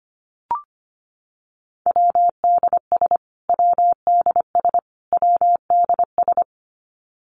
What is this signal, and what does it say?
Morse code, heard as a keyed tone, about 25 wpm, 700 Hz. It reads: WDS WDS WDS
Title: Abbreviations Repeated 3x - 25wpm